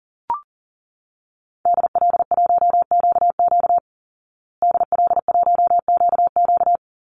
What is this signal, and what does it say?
Morse code, heard as a keyed tone, about 40 wpm, 700 Hz. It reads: DL1QQ DL1QQ
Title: Call Signs Repeated 3x - 40wpm